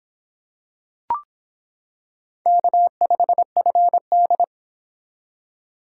Morse code, heard as a keyed tone, about 26 wpm, 700 Hz. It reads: K5FD